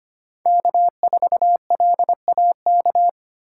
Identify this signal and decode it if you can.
Morse code, heard as a keyed tone, about 25 wpm, 700 Hz. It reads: K4LAK
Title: USA Call Signs - 6x Character Spacing - 25wpm